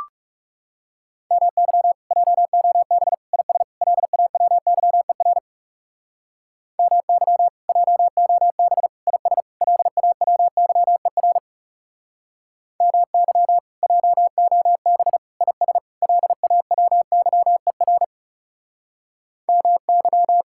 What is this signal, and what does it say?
Morse code, heard as a keyed tone, about 39 wpm, 700 Hz. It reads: MY JOB IS LAWYER MY JOB IS LAWYER MY JOB IS LAWYER MY